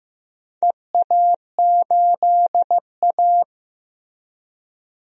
Morse code, heard as a keyed tone, about 15 wpm, 700 Hz. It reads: EA8A
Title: DXCC-100 Call Signs - 4 Characters - Repeated 3x - 15wpm